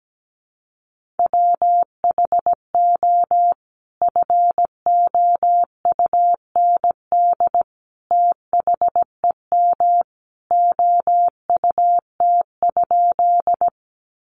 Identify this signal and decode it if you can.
Morse code, heard as a keyed tone, about 17 wpm, 700 Hz. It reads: WHO FOUND THEM OUT?
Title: Sentences from Top 300 Words 17wpm